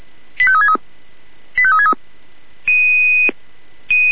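Five-tone selective call (ZVEI), heard as quick runs of five tones, tones of about 70 ms. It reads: 85363 85363
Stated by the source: ZVEI five-tone selective call received on an SDR.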